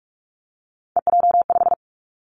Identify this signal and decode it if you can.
Morse code, heard as a keyed tone, about 46 wpm, 700 Hz. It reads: EJ5
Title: Three Challenging Letter-Number Combinations 45wpm